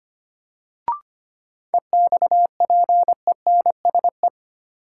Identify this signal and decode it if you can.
Morse code, heard as a keyed tone, about 25 wpm, 700 Hz. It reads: EXPENSE
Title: Sets of 4 Words - New Word Drill - 25wpm